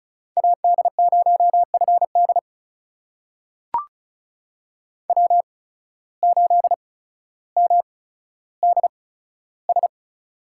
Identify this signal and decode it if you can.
Morse code, heard as a keyed tone, about 34 wpm, spaced as for 15 wpm, 700 Hz. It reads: AD0FD W8MDS
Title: USA Call Signs - 5 Characters - 8x Character Spacing - 35wpm